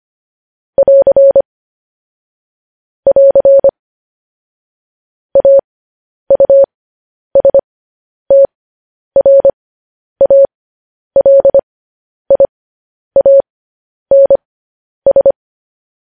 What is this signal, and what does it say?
Morse code, heard as a keyed tone, about 25 wpm, spaced as for 10 wpm, 550 Hz. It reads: <AR> <AR> AUSTRALIANS
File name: podcast / News headlines-1x-25x10